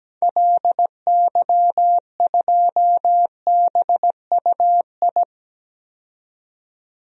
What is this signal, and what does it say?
Morse code, heard as a keyed tone, about 17 wpm, 700 Hz. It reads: LY2BUI
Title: DXCC-100 Call Signs - 6 Characters - 17wpm